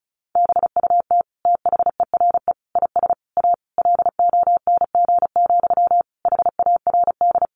Morse code, heard as a keyed tone, about 34 wpm, 700 Hz. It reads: BUT THERE IS A LONG, HARD
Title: Sentences from Top 1100 Words 35wpm